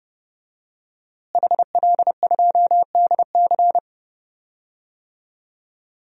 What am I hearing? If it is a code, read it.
Morse code, heard as a keyed tone, about 30 wpm, 700 Hz. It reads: HL2DC
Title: DXCC-100 Call Signs - 5 Characters - Repeated 2x - 30wpm